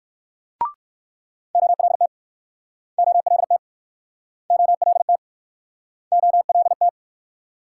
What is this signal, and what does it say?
Morse code, heard as a keyed tone, about 64 wpm, 690 Hz. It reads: OLT OLT OLT OLT